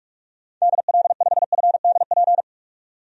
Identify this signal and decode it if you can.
Morse code, heard as a keyed tone, about 41 wpm, 700 Hz. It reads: DL5FDP
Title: DXCC-100 Call Signs - 6 Characters - 45wpm